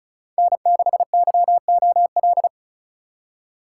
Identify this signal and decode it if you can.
Morse code, heard as a keyed tone, about 46 wpm, 700 Hz. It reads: N6YOL